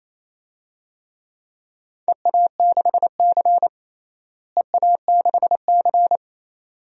Morse code, heard as a keyed tone, about 28 wpm, 700 Hz. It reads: EA6C EA6C
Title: DXCC-100 Call Signs - 4 Characters - Repeated 2x - 28wpm